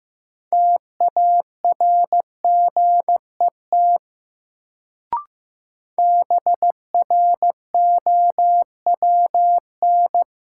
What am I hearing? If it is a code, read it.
Morse code, heard as a keyed tone, about 15 wpm, 700 Hz. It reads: TARGET BROWN
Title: Sentences from Top 1200 Words - New Word Drill Alt - 15wpm